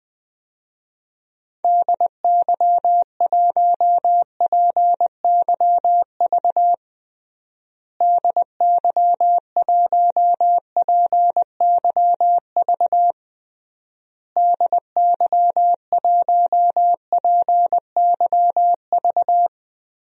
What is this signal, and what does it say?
Morse code, heard as a keyed tone, about 21 wpm, 700 Hz. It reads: DY1PYV DY1PYV DY1PYV
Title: DXCC-100 Call Signs - 6 Characters - Repeated 3x - 20wpm